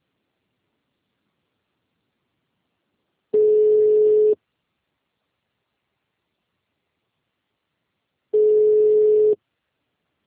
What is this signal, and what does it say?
Ringback tone (Europe, 425 Hz).
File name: out- -2007- - - .2619